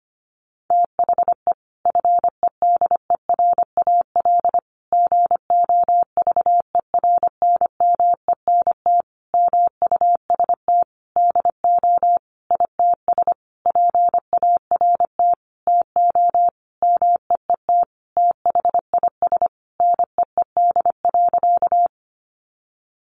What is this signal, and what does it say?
Morse code, heard as a keyed tone, about 25 wpm, 700 Hz. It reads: THE FEDERAL GOVERNMENT MUST DO ITS PART TO MEET THIS NEED.
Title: Sentences from Top 1100 Words 25wpm